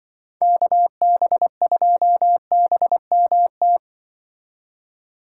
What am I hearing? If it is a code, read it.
Morse code, heard as a keyed tone, about 24 wpm, 700 Hz. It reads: KB2BMT
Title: Speed Racing - USA 2x3 Call Signs 20wpm